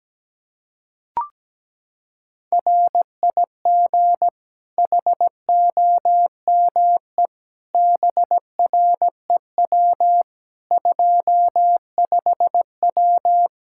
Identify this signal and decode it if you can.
Morse code, heard as a keyed tone, about 17 wpm, 700 Hz. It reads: RIG HOME BREW 25W
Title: K1USN SST QSO Practice 17wpm